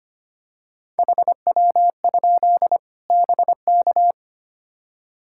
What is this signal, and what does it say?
Morse code, heard as a keyed tone, about 25 wpm, 700 Hz. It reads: HW? BK
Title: Practice QSO - Exchange 1 - Beginning - 25wpm